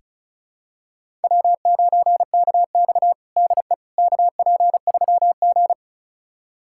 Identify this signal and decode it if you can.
Morse code, heard as a keyed tone, about 36 wpm, 700 Hz. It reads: W9KX DE KP3G
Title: USA Callsign DE Callsign - 4 Characters - 35wpm